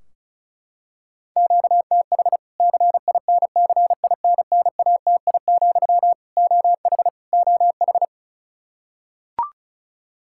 Morse code, heard as a keyed tone, about 36 wpm, 700 Hz. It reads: QTH CINCINNATI, OH OH
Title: K1USN SST QSO Practice 35wpm